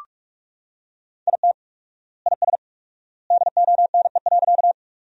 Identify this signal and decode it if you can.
Morse code, heard as a keyed tone, about 43 wpm, 690 Hz. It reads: IT IS DONE.
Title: 4x Word-Spacing - Sentences from Top 500 Words - 45wpm